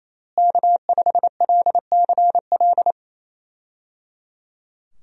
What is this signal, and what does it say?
Morse code, heard as a keyed tone, about 27 wpm, 700 Hz. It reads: K5LCL
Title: USA Call Signs - 5 Characters - Repeated 3x - 28wpm